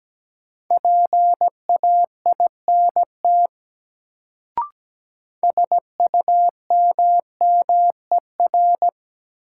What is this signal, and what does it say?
Morse code, heard as a keyed tone, about 17 wpm, 700 Hz. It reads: PAINT SUMMER
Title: Sentences from Top 1000 Words - Review Part 6 - New Word Drill - 17wpm